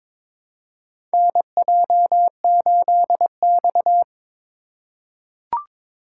Morse code, heard as a keyed tone, about 22 wpm, 700 Hz. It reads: NJ8X
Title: USA Call Signs - 4 Characters - 3x Character Spacing - 22wpm